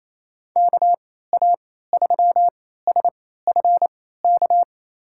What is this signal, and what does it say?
Morse code, heard as a keyed tone, about 28 wpm, spaced as for 16 wpm, 700 Hz. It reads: KA3SFK
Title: USA Call Signs - 3x Character Spacing - 28wpm